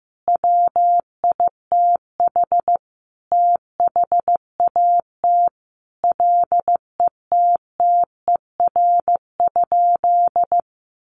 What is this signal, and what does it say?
Morse code, heard as a keyed tone, about 15 wpm, 700 Hz. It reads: WITH THAT LETTER?
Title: Sentences from Top 300 Words 15wpm